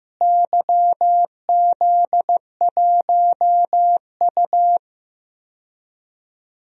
Morse code, heard as a keyed tone, about 15 wpm, 700 Hz. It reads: YZ1U
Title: DXCC-100 Call Signs - 4 Characters - Repeated 2x - 15wpm